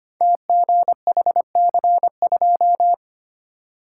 Morse code, heard as a keyed tone, about 25 wpm, 700 Hz. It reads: TGHC2
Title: Code Groups - 5 Letter:Numbers - 25wpm